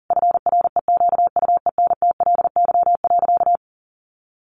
Morse code, heard as a keyed tone, about 40 wpm, 700 Hz. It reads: FREQUENTLY.
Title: Sentences from Top 1300 Words 40wpm